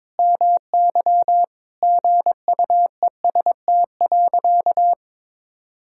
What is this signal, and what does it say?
Morse code, heard as a keyed tone, about 22 wpm, 700 Hz. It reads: MY GUEST.
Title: Sentences from Top 1600 Words 22wpm